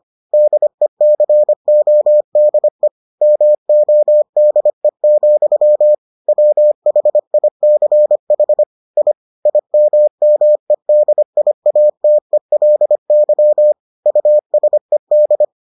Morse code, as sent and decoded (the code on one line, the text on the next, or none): -.. -.-. --- -.. / -- --- -.. --..-- / .-- .... .. -.-. .... / .. / .. -- -- . -.. .. .- - . .-.. -.-- / ..- ... -..
DECODE MODE, WHICH I IMMEDIATELY USED